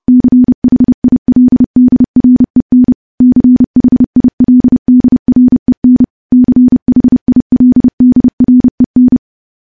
Morse code, heard as a keyed tone, about 30 wpm, 250 Hz. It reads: CHILDREN CHILDREN CHILDREN
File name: morse / morse-training-30WPM